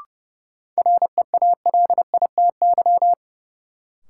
Morse code, heard as a keyed tone, about 29 wpm, 700 Hz. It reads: REALITY
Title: Sentences from Top 1200 Words - New Word Drill Alt - 30wpm